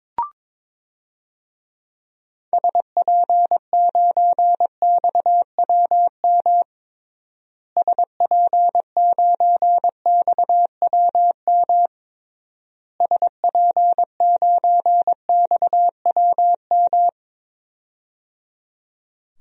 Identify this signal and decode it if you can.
Morse code, heard as a keyed tone, about 22 wpm, 700 Hz. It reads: SP9XWM SP9XWM SP9XWM
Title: DXCC-100 Call Signs - 6 Characters - Repeated 3x - 22wpm